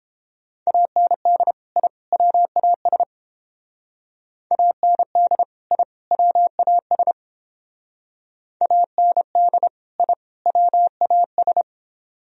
Morse code, heard as a keyed tone, about 29 wpm, 700 Hz. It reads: AND I WAS AND I WAS AND I WAS